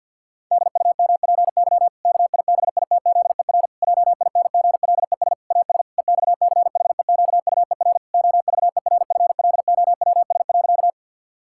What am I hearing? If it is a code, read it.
Morse code, heard as a keyed tone, about 50 wpm, 700 Hz. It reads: DUMPY KIBITZER JINGLES AS EXCHEQUER OVERFLOWS.
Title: Pangram Challenge v2 - 50wpm